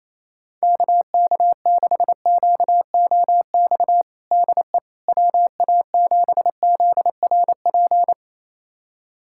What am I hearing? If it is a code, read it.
Morse code, heard as a keyed tone, about 28 wpm, 700 Hz. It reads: KK6QOX DE WA7ZRP
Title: USA Callsign DE Callsign - 6 Characters - 28wpm